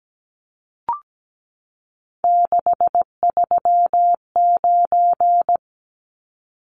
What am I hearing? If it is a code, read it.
Morse code, heard as a keyed tone, about 17 wpm, 700 Hz. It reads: 639
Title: Numbers - 3-Digit 17wpm